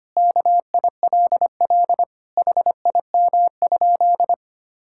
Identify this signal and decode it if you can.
Morse code, heard as a keyed tone, about 25 wpm, 700 Hz. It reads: KILL HIM?
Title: Sentences from Top 1600 Words 25wpm